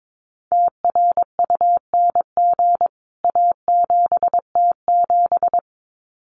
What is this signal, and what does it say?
Morse code, heard as a keyed tone, about 22 wpm, 700 Hz. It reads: TRUNG A7T7
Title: CWT Exchange 22wpm